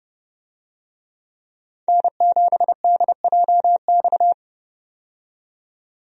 Morse code, heard as a keyed tone, about 30 wpm, 700 Hz. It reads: N7DJX
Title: USA Call Signs - 5 Characters - 30wpm